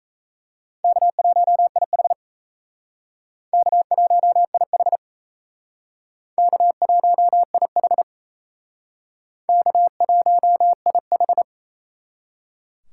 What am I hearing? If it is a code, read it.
Morse code, heard as a keyed tone, about 35 wpm, 700 Hz. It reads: K1IH K1IH K1IH K1IH
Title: Speed Racing - USA 1x2 Call Signs 28wpm